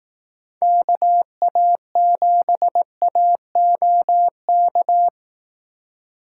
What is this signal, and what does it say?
Morse code, heard as a keyed tone, about 18 wpm, 700 Hz. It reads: KA7AOK
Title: Speed Racing - USA Call Signs 15wpm